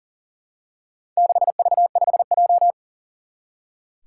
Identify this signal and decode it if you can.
Morse code, heard as a keyed tone, about 40 wpm, 700 Hz. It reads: 6V5J
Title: Four Challenging Letter-Number Combinations 40wpm